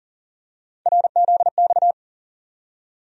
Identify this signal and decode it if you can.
Morse code, heard as a keyed tone, about 40 wpm, 700 Hz. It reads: RZX
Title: USA Call Sign Postfixes - 3 Letters 40wpm